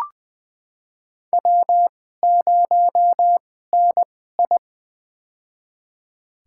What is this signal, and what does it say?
Morse code, heard as a keyed tone, about 20 wpm, spaced as for 14 wpm, 700 Hz. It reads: W0NI
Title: USA Call Signs - 4 Characters - 2x Character Spacing - 20wpm